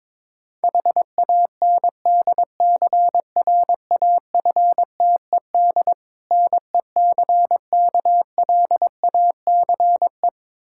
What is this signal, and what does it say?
Morse code, heard as a keyed tone, about 22 wpm, 700 Hz. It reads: HANDCRAFTED NECKLACE
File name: Difficult Words with Hints - 22wpm